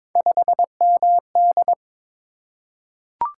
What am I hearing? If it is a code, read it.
Morse code, heard as a keyed tone, about 22 wpm, 700 Hz. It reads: 5MD
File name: practice / 22wpm / Three Letter-Number Combinations - 22wpm